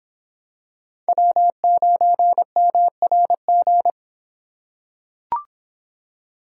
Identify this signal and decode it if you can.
Morse code, heard as a keyed tone, about 25 wpm, 700 Hz. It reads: W9MRG